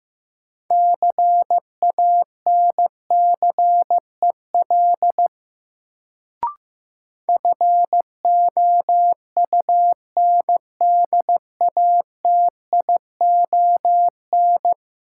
Morse code, heard as a keyed tone, about 16 wpm, 700 Hz. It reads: CANCEL FOUNDATION
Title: Sentences from Top 1200 Words - New Word Drill Alt - 15wpm